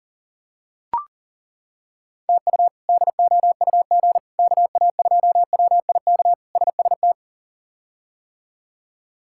Morse code, heard as a keyed tone, about 39 wpm, 700 Hz. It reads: TU DOUG KA2WIK SST
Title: K1USN SST QSO Practice - 40wpm